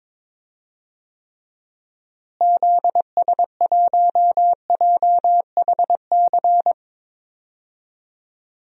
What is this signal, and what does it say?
Morse code, heard as a keyed tone, about 21 wpm, 700 Hz. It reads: ZS1JHC
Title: DXCC-100 Call Signs - 6 Characters - Repeated 3x - 22wpm